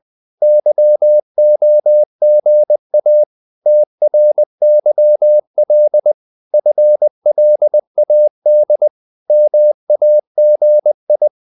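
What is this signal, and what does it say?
Morse code, heard as a keyed tone, about 20 wpm, 600 Hz. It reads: YOGA TRYL FLAD MAGI
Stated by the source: Morse code practice of short Danish words at 20 wpm, from CW med Gnister.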